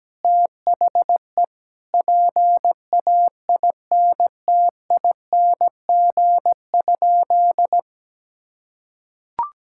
Morse code, heard as a keyed tone, about 17 wpm, 700 Hz. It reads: THE PAINTING?
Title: Sentences from Top 1000 Words - Review Part 6 - 17wpm